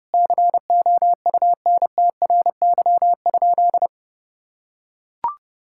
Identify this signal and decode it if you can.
Morse code, heard as a keyed tone, about 32 wpm, 700 Hz. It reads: COUNTRY?
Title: Sentences from Top 1000 Words - Review Part 5 - 30wpm